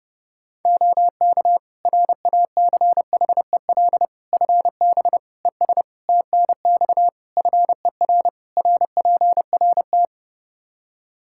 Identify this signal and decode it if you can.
Morse code, heard as a keyed tone, about 30 wpm, 700 Hz. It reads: OK RACHEL FB ES TNX FER RPRT
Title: Practice QSO - Exchange 2 - Info - 30wpm